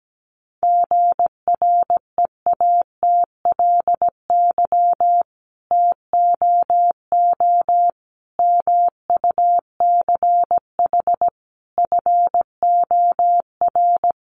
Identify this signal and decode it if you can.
Morse code, heard as a keyed tone, about 17 wpm, 700 Hz. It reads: GREATLY TOO MUCH FOR
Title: Sentences from Top 1400 Words 17wpm